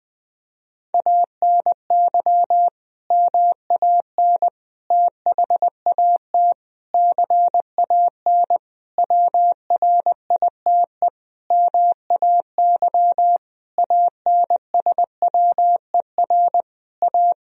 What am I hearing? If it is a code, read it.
Morse code, heard as a keyed tone, about 20 wpm, 700 Hz. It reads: ANY MAN THAT CAN WRITE MAY ANSWER A
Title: Sentences from Top 1000 Words - Review Part 1 - 20wpm